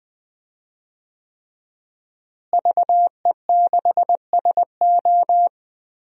Morse code, heard as a keyed tone, about 20 wpm, 700 Hz. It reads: VE6SO
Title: DXCC-100 Call Signs - 5 Characters - Repeated 3x - 20wpm